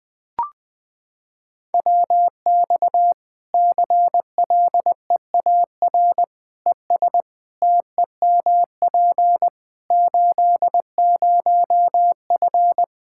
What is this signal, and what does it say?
Morse code, heard as a keyed tone, about 21 wpm, 700 Hz. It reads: WX CLEAR ES TEMP 80F
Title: K1USN SST QSO Practice 20wpm